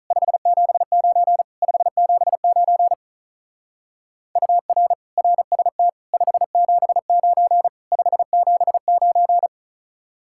Morse code, heard as a keyed tone, about 35 wpm, 700 Hz. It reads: 579 579 UR RST 579 579
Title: Speed-Racing Common QSO Phrases Encore - 35wpm